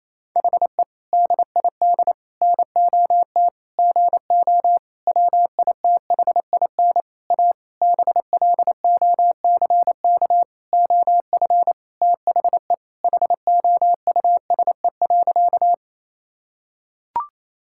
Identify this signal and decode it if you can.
Morse code, heard as a keyed tone, about 28 wpm, 700 Hz. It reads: HE DID NOT GO WITHIN A BLOCK OF THE HOUSE.
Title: Sentences from Top 1600 Words 28wpm